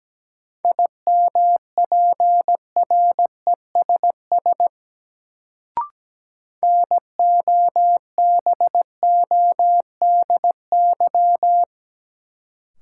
Morse code, heard as a keyed tone, about 17 wpm, 700 Hz. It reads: IMPRESS NOBODY
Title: Sentences from Top 1600 Words - New Word Drill Alt - 17wpm